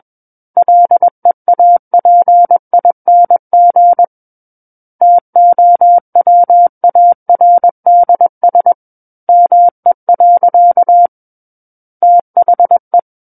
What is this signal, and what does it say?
Morse code, heard as a keyed tone, about 21 wpm, 700 Hz. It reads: LEAPING TOWARDS ME. THE
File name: Book One / 12-Chapter 12 - 21 WPM